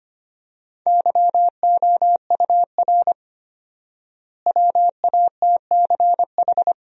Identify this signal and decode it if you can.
Morse code, heard as a keyed tone, about 25 wpm, 700 Hz. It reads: YOUR WATCH
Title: 4x Word-Spacing - Sentences from Top 400 Words - 25wpm